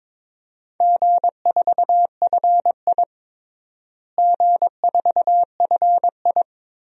Morse code, heard as a keyed tone, about 22 wpm, 700 Hz. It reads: G4FI G4FI
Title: DXCC-100 Call Signs - 4 Characters - Repeated 3x - 22wpm